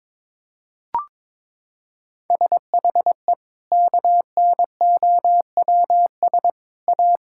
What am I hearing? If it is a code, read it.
Morse code, heard as a keyed tone, about 22 wpm, 700 Hz. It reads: SHE KNOWS A
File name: practice / 22wpm / Sentences with 4-Word Phrases - 22wpm